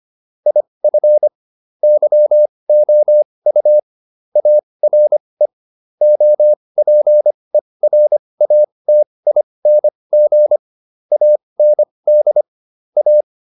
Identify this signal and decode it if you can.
Morse code, heard as a keyed tone, about 25 wpm, spaced as for 20 wpm, 600 Hz. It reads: IF YOU ARE OPERATING AND A